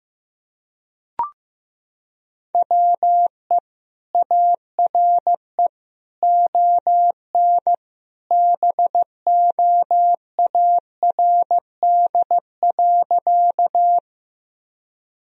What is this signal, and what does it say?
Morse code, heard as a keyed tone, about 15 wpm, 700 Hz. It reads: WE ARE ON BOARD.
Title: Sentences from Top 1000 Words - Review Part 4 - 15wpm